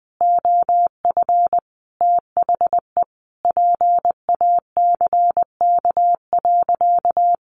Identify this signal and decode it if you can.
Morse code, heard as a keyed tone, about 20 wpm, 700 Hz. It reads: OF THE PACK.
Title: Sentences from Top 1300 Words 20wpm